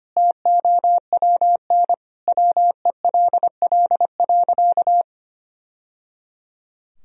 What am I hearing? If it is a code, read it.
Morse code, heard as a keyed tone, about 25 wpm, 700 Hz. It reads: TOWN WELL.
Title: Sentences from Top 1000 Words - Review Part 2 - 25wpm